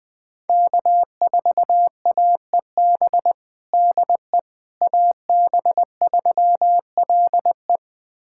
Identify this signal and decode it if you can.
Morse code, heard as a keyed tone, about 19 wpm, 700 Hz. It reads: K4AEB DE AB3LE
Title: USA Callsign DE Callsign - 5 Characters - 20wpm